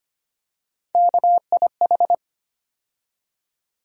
Morse code, heard as a keyed tone, about 25 wpm, 700 Hz. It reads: KIH
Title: USA Call Sign Postfixes - 3 Letters 25wpm